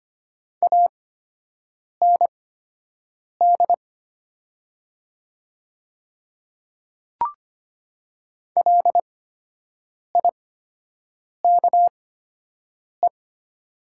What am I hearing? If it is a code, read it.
Morse code, heard as a keyed tone, about 25 wpm, spaced as for 7 wpm, 690 Hz. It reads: AND LIKE
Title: 8x Character Spacing - Top 300 Words - 25wpm